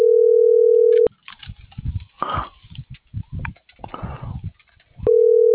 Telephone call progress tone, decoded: ringback tone

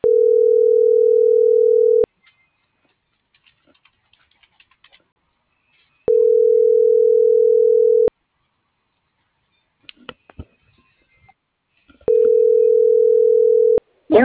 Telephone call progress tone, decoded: ringback tone

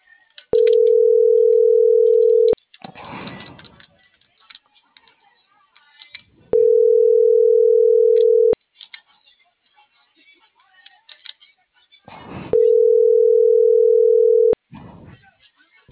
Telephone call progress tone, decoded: ringback tone

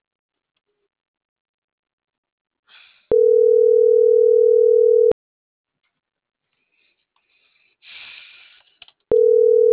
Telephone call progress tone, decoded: ringback tone

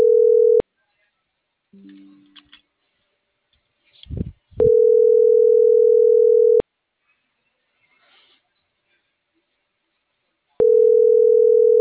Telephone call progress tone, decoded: ringback tone

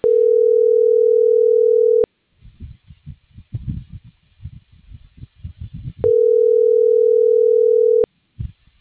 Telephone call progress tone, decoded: ringback tone

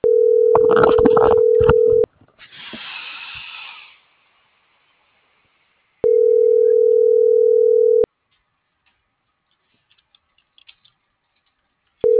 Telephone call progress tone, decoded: ringback tone

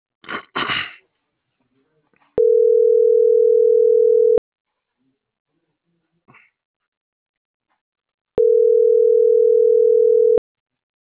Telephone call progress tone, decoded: ringback tone

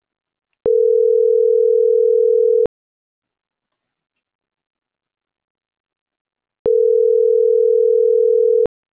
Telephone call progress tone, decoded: ringback tone

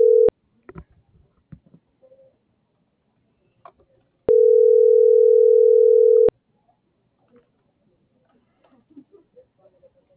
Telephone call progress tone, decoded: ringback tone

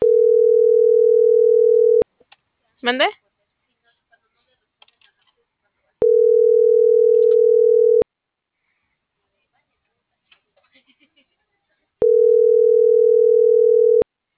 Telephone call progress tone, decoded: ringback tone

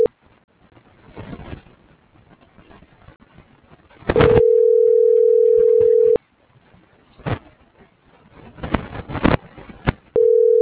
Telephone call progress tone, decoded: ringback tone

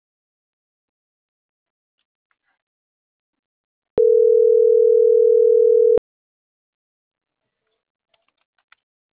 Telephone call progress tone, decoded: ringback tone

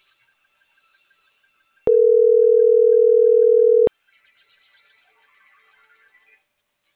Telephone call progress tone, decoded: ringback tone

